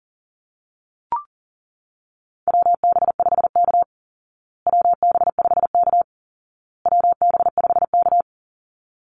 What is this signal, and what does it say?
Morse code, heard as a keyed tone, about 40 wpm, 700 Hz. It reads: WB5K WB5K WB5K
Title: Call Signs Repeated 2x - 40wpm